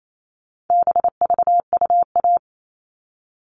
Morse code, heard as a keyed tone, about 28 wpm, 700 Hz. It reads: BVUA
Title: All Cut Numbers - 4-Digit 28wpm